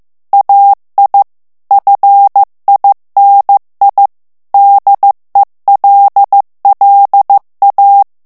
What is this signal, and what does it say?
Morse code, heard as a keyed tone, about 15 wpm, 800 Hz. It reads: AI FINI DELLA
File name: morse mp3 / ITALIEN texte clair 2 vitesse 14 mots minute